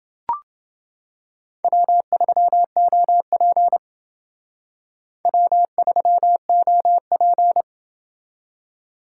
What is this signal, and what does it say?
Morse code, heard as a keyed tone, about 29 wpm, 700 Hz. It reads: W3OP W3OP